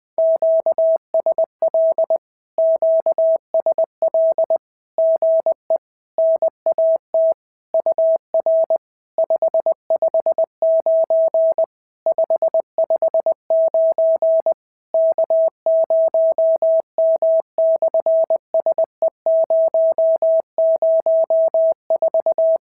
Morse code, heard as a keyed tone, about 20 wpm, 650 Hz. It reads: QSL QSL GE NAT UR 559 559 K0M/SE004
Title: SOTA QSO Practice 20wpm